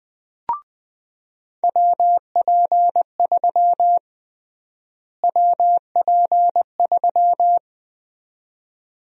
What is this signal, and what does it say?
Morse code, heard as a keyed tone, about 20 wpm, 700 Hz. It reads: WP3 WP3
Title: USA Call Sign Prefixes - Repeated 2x - 20wpm